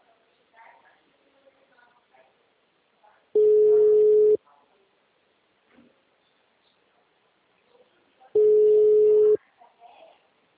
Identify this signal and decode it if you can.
Ringback tone (Europe, 425 Hz), with speech.
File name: out- -2055- - - .26324